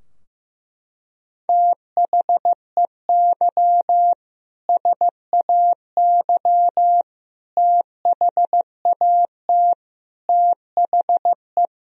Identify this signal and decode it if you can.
Morse code, heard as a keyed tone, about 15 wpm, 700 Hz. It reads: THEY SAY THAT THE
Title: Sentences from Top 1000 Words - Review Part 3 - 15wpm